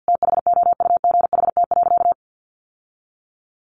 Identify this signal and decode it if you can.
Morse code, heard as a keyed tone, about 50 wpm, 700 Hz. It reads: THOUGHT.
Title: Sentences from Top 1300 Words 50wpm